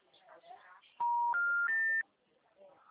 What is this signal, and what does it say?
Special information tone, with speech.
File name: out- -2009- - - .2